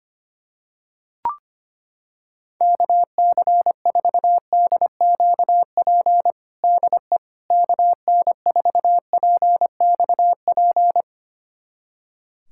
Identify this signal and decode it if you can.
Morse code, heard as a keyed tone, about 25 wpm, 700 Hz. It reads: KC4DQP DE KN4PXP
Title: USA Callsign DE Callsign 25wpm